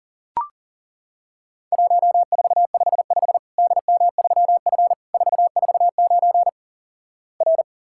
Morse code, heard as a keyed tone, about 37 wpm, 700 Hz. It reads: 1455 DM3F 449 R
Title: Worked All Europe DX Contest - QTC Exchange - 40wpm